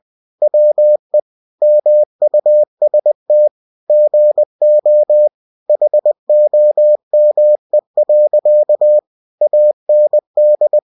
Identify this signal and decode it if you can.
Morse code, heard as a keyed tone, about 20 wpm, 600 Hz. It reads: WE MUST GO HOME. AND
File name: cw-books / winnie-the-pooh / winnie-20wpm-page-0066